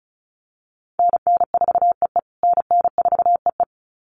Morse code, heard as a keyed tone, about 34 wpm, 700 Hz. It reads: NN4EE NN4EE
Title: USA Call Signs Repeated 2x 35wpm